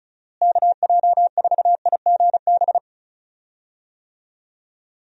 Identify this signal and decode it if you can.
Morse code, heard as a keyed tone, about 36 wpm, 700 Hz. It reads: KJ4IGB
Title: USA Call Signs - 6 Characters - Repeated 3x - 35wpm